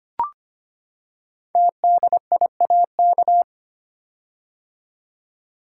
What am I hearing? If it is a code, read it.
Morse code, heard as a keyed tone, about 25 wpm, 700 Hz. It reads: TDIAK
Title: Code Groups - 5 Letters - 25wpm